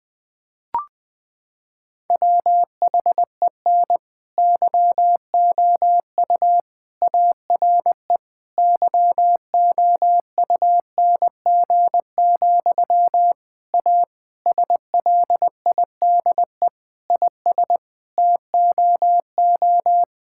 Morse code, heard as a keyed tone, about 20 wpm, 700 Hz. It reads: WHEN YOU ARE YOUNG, A SLIDE IS TOO
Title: Sentences from Top 1600 Words 20wpm